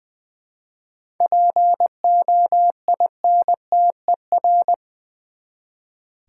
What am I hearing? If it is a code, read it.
Morse code, heard as a keyed tone, about 20 wpm, 700 Hz. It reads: POINTER
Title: Easy Words with Hints - 20wpm